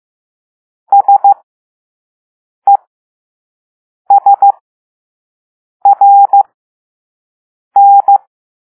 Morse code, heard as a keyed tone, about 15 wpm, spaced as for 5 wpm, 800 Hz. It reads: SESRN